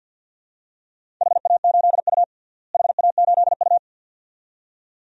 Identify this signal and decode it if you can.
Morse code, heard as a keyed tone, about 48 wpm, 700 Hz. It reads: HA8U HA8U
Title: DXCC-100 Call Signs - 4 Characters - Repeated 2x - 50wpm